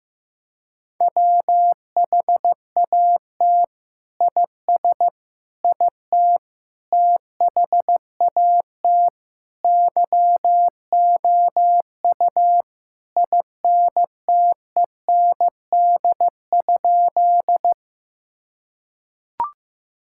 Morse code, heard as a keyed tone, about 16 wpm, 700 Hz. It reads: WHAT IS IT THAT YOU INTEND?
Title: Sentences from Top 1600 Words 15wpm